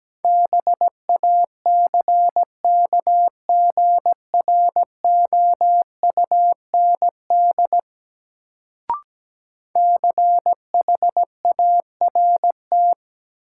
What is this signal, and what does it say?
Morse code, heard as a keyed tone, about 17 wpm, 700 Hz. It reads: BACKGROUND CHART
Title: Sentences from Top 1300 Words - New Word Drill Alt - 17wpm